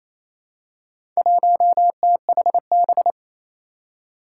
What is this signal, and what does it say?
Morse code, heard as a keyed tone, about 28 wpm, 700 Hz. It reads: 1THB
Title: Four Challenging Letter-Number Combinations 28wpm